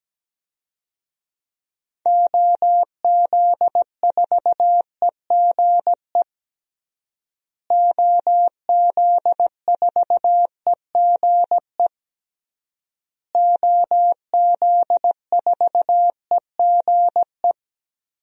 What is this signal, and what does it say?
Morse code, heard as a keyed tone, about 17 wpm, 700 Hz. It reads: OZ4EGE OZ4EGE OZ4EGE
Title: DXCC-100 Call Signs - 6 Characters - Repeated 3x - 17wpm